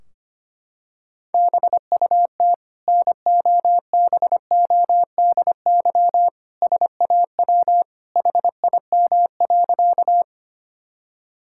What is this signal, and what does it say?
Morse code, heard as a keyed tone, about 25 wpm, 700 Hz. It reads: BUT NOBODY SAW HIM.
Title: Sentences from Top 1600 Words 25wpm